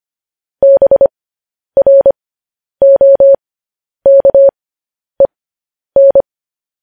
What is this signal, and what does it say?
Morse code, heard as a keyed tone, about 25 wpm, spaced as for 10 wpm, 550 Hz. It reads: BROKEN
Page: news-headlines-in-morse-code-at-10-wpm